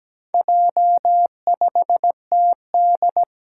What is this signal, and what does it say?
Morse code, heard as a keyed tone, about 17 wpm, 700 Hz. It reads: J5TD
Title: Four Challenging Letter-Number Combinations 17wpm